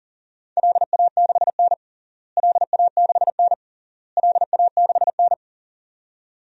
Morse code, heard as a keyed tone, about 37 wpm, 700 Hz. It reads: LA6N LA6N LA6N
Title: DXCC-100 Call Signs - 4 Characters - Repeated 3x - 40wpm